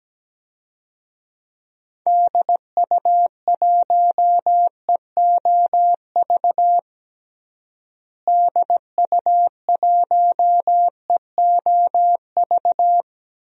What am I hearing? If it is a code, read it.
Morse code, heard as a keyed tone, about 17 wpm, 700 Hz. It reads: DU1EOV DU1EOV
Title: DXCC-100 Call Signs - 6 Characters - Repeated 2x - 17wpm